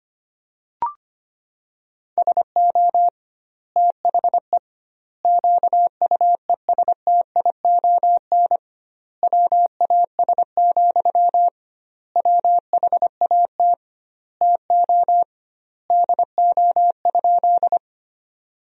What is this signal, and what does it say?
Morse code, heard as a keyed tone, about 25 wpm, 700 Hz. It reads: SO THE QUESTION WAS, WHAT TO DO?
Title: 2x Word-Spacing - Sentences from Top 400 Words - 25wpm